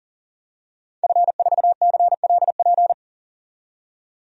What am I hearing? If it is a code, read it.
Morse code, heard as a keyed tone, about 40 wpm, 700 Hz. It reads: F4CLP
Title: DXCC-100 Call Signs - 5 Characters - 40wpm